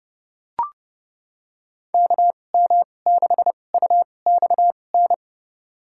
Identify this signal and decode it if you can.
Morse code, heard as a keyed tone, about 30 wpm, spaced as for 22 wpm, 700 Hz. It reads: KM6UXN
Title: USA Call Signs - 6 Characters - 2x Character Spacing - 30wpm